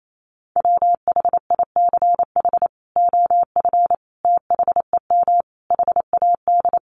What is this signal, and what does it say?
Morse code, heard as a keyed tone, about 28 wpm, 700 Hz. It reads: WHICH OF THEM HAD
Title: Sentences from Top 1100 Words 28wpm